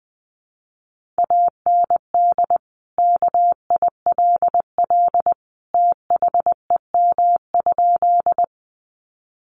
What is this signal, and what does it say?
Morse code, heard as a keyed tone, about 20 wpm, 700 Hz. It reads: AND KILL THEM?
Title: Sentences from Top 1400 Words 20wpm